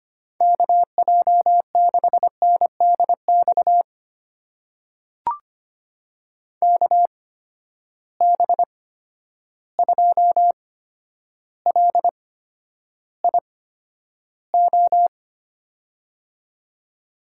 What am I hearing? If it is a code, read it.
Morse code, heard as a keyed tone, about 25 wpm, spaced as for 11 wpm, 700 Hz. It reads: KJ6NDX KB2LIO
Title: USA Call Signs - 6 Characters - 8x Character Spacing - 25wpm